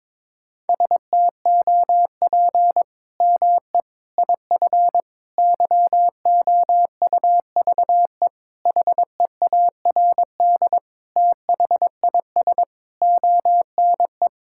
Morse code, heard as a keyed tone, about 22 wpm, 700 Hz. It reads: STOP ME IF YOUVE HEARD THIS ONE
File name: practice / 22wpm / Sets of 8 Words - 22wpm